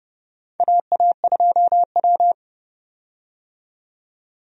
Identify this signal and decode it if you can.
Morse code, heard as a keyed tone, about 30 wpm, 700 Hz. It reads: AA2W
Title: USA Call Signs - 4 Characters - 30wpm